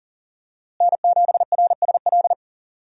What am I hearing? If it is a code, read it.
Morse code, heard as a keyed tone, about 37 wpm, 700 Hz. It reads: N7RSL
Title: USA Call Signs - 5 Characters - Repeated 3x - 40wpm